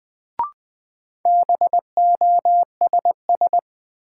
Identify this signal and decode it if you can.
Morse code, heard as a keyed tone, about 20 wpm, 700 Hz. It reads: BOSS
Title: Sentences from Top 700 Words - New Word Drill - 20wpm